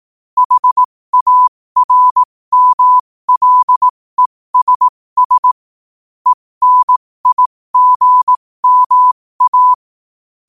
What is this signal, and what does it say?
Morse code, heard as a keyed tone, about 18 wpm, spaced as for 15 wpm, 1000 Hz. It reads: HARMLESS ENIGMA
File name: qotdfiles / 15wpm